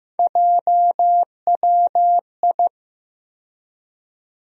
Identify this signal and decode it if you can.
Morse code, heard as a keyed tone, about 15 wpm, 700 Hz. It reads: JWI
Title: USA Call Sign Postfixes - 3 Letters 15wpm